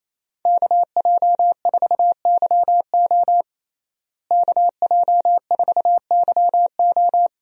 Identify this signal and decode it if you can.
Morse code, heard as a keyed tone, about 28 wpm, 700 Hz. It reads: KJ4YO KJ4YO
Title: USA Call Signs - 5 Characters - Repeated 2x - 28wpm